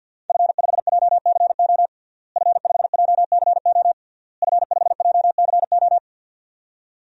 Morse code, heard as a keyed tone, about 51 wpm, 700 Hz. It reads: F5JCO F5JCO F5JCO
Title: DXCC-100 Call Signs - 5 Characters - Repeated 3x - 50wpm